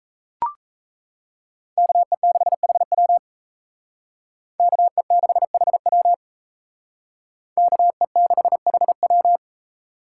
Morse code, heard as a keyed tone, about 36 wpm, 700 Hz. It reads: KE6HW KE6HW KE6HW